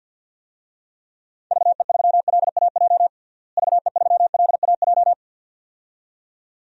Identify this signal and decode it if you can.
Morse code, heard as a keyed tone, about 51 wpm, 700 Hz. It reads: VE3LAJ VE3LAJ
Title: DXCC-100 Call Signs - 6 Characters - Repeated 2x - 50wpm